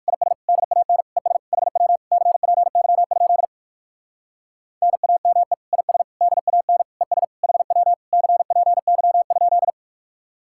Morse code, heard as a keyed tone, about 59 wpm, 700 Hz. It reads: IS DAN ES HW CPY? NAME IS DAN ES HW CPY?